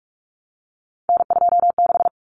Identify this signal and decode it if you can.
Morse code, heard as a keyed tone, about 44 wpm, 700 Hz. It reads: N26
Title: Common Cut Numbers - 3-Digit 45wpm